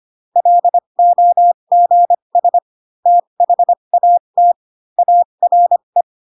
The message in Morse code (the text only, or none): LOGS THAT ARE